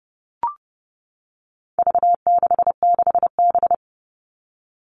Morse code, heard as a keyed tone, about 30 wpm, 700 Hz. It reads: V66B
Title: All Cut Numbers - 4-Digit 30wpm